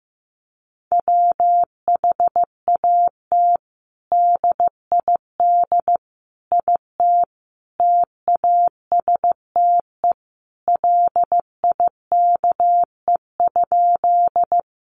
Morse code, heard as a keyed tone, about 15 wpm, 700 Hz. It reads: WHAT DID IT TASTE LIKE?
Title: Sentences from Top 1100 Words 15wpm